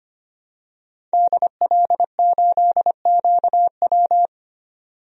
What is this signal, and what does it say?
Morse code, heard as a keyed tone, about 25 wpm, 700 Hz. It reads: DL8QW
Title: DXCC-100 Call Signs - 5 Characters - 25wpm